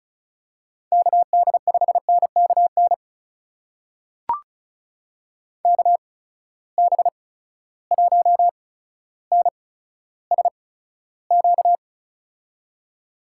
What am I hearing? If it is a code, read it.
Morse code, heard as a keyed tone, about 36 wpm, spaced as for 15 wpm, 700 Hz. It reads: KD5NKN KB1NSQ
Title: USA Call Signs - 6 Characters - 8x Character Spacing - 35wpm